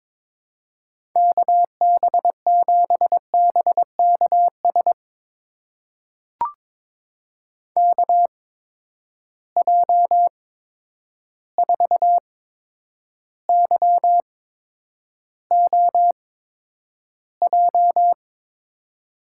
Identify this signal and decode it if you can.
Morse code, heard as a keyed tone, about 23 wpm, spaced as for 10 wpm, 700 Hz. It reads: KB7BKS KJ4YOJ
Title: USA Call Signs - 6 Characters - 8x Character Spacing - 22wpm